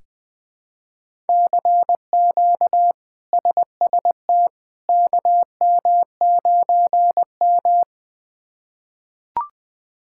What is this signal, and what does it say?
Morse code, heard as a keyed tone, about 20 wpm, 700 Hz. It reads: CQ SST KM9M
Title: K1USN SST QSO Practice - 20wpm